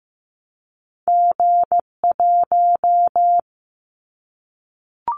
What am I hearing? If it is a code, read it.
Morse code, heard as a keyed tone, about 15 wpm, 700 Hz. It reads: G1
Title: Callsign Prefixes for Top 25 DXCC - 15wpm